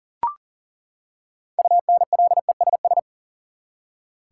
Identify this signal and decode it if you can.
Morse code, heard as a keyed tone, about 38 wpm, 700 Hz. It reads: UNLESS
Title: Six Letter Words - 40wpm